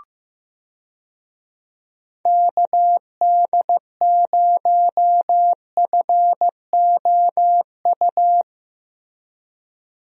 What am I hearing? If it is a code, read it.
Morse code, heard as a keyed tone, about 15 wpm, 700 Hz. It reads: KD0FOU
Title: USA Call Signs - 6 Characters - Repeated 2x - 15wpm